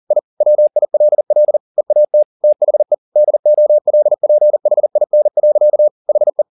Morse code, heard as a keyed tone, about 40 wpm, 600 Hz. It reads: I WILL EAT THE DOLPHIN. HE
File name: the old man and the sea 2 40wpm